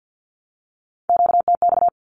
Morse code, heard as a keyed tone, about 50 wpm, 700 Hz. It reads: QTX
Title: Q-Codes 50wpm